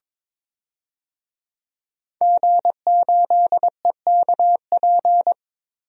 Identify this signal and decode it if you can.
Morse code, heard as a keyed tone, about 21 wpm, 700 Hz. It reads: G8EKP
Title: DXCC-100 Call Signs - 5 Characters - Repeated 3x - 22wpm